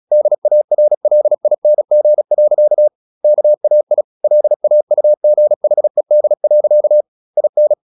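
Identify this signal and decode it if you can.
Morse code, heard as a keyed tone, about 37 wpm, 600 Hz. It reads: DARLING. KAI LAUGHED. IN
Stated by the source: Morse practice at 36 wpm, text of Ken Follett's novel Never, from CW med Gnister.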